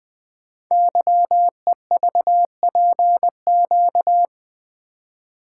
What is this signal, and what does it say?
Morse code, heard as a keyed tone, about 20 wpm, 700 Hz. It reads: YEVPQ
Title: Code Groups - 5 Letters - 20wpm